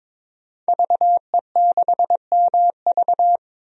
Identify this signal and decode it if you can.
Morse code, heard as a keyed tone, about 22 wpm, 700 Hz. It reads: VE6MV
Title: DXCC-100 Call Signs - 5 Characters - Repeated 3x - 22wpm